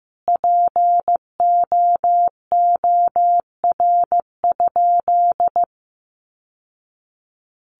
Morse code, heard as a keyed tone, about 15 wpm, 700 Hz. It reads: POOR?
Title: Sentences from Top 1100 Words 15wpm